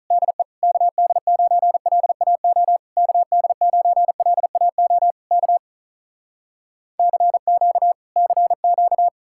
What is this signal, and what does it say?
Morse code, heard as a keyed tone, about 46 wpm, 700 Hz. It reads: DE KD9LAO KD9LAO K CQ CQ